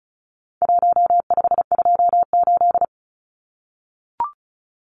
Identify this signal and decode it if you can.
Morse code, heard as a keyed tone, about 34 wpm, 700 Hz. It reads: 1528
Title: Numbers - Contest Style - 4-Digit 35wpm